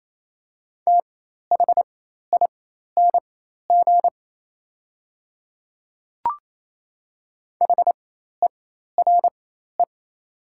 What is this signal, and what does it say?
Morse code, heard as a keyed tone, about 28 wpm, spaced as for 13 wpm, 700 Hz. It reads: THING HERE
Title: 4x Character Spacing - Top 500 Words - 28wpm